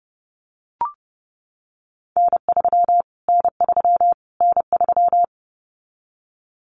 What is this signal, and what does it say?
Morse code, heard as a keyed tone, about 30 wpm, 700 Hz. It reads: N3 N3 N3
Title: Callsign Prefixes for Top 25 DXCC - Repeated 3x - 30wpm